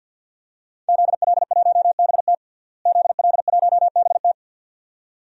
Morse code, heard as a keyed tone, about 52 wpm, 700 Hz. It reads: ZL1BT ZL1BT
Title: DXCC-100 Call Signs - 5 Characters - Repeated 2x - 50wpm